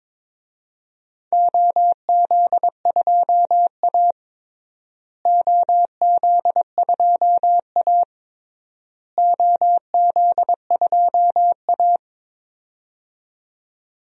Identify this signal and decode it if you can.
Morse code, heard as a keyed tone, about 22 wpm, 700 Hz. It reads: OZ2A OZ2A OZ2A
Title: DXCC-100 Call Signs - 4 Characters - Repeated 3x - 22wpm